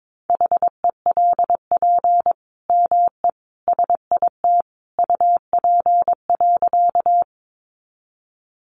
Morse code, heard as a keyed tone, about 22 wpm, 700 Hz. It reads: HELP ME SIT UP.
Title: Sentences from Top 500 Words 22wpm